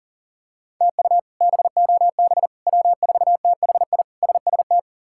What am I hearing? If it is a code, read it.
Morse code, heard as a keyed tone, about 37 wpm, 700 Hz. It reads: TU BOB W4THI SST
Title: K1USN SST QSO Practice - 40wpm